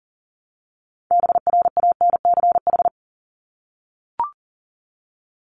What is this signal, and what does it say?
Morse code, heard as a keyed tone, about 40 wpm, 700 Hz. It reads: BRANCH
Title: Top 1751-2000 Words 40wpm